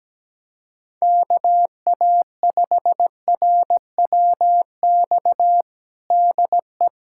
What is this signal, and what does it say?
Morse code, heard as a keyed tone, about 17 wpm, 700 Hz. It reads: KA5RWX DE
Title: USA Callsign DE Callsign - 6 Characters - 17wpm